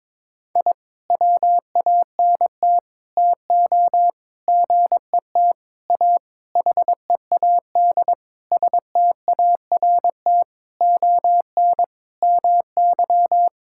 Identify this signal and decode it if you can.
Morse code, heard as a keyed tone, about 22 wpm, 700 Hz. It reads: I WANT TO GET A HEAD START ON MY
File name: Sentences with 9-Word Phrases - 22wpm